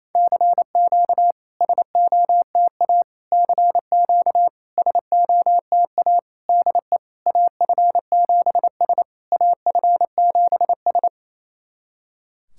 Morse code, heard as a keyed tone, about 28 wpm, 700 Hz. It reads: CQ SOTA CQ SOTA DE AF7S AF7S
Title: SOTA QSO Practice 28wpm